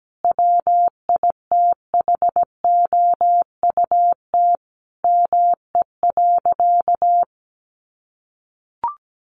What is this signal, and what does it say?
Morse code, heard as a keyed tone, about 17 wpm, 700 Hz. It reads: WITHOUT ME.
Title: Sentences from Top 1200 Words 17wpm